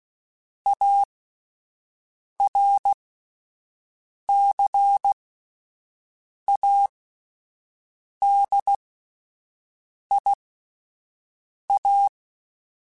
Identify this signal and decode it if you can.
Morse code, heard as a keyed tone, about 16 wpm, spaced as for 5 wpm, 800 Hz. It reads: ARCADIA
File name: fromSMs / FromSMs mp3s / Cities@5